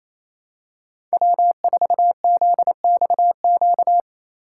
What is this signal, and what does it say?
Morse code, heard as a keyed tone, about 28 wpm, 700 Hz. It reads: W4ZXQ
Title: USA Call Signs - 5 Characters - 2x Character Spacing - 28wpm